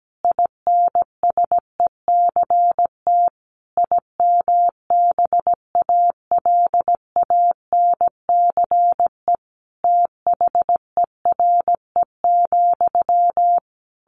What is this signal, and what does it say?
Morse code, heard as a keyed tone, about 17 wpm, 700 Hz. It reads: INSECT IMBALANCE THERE,
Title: A Sound Of Thunder 17wpm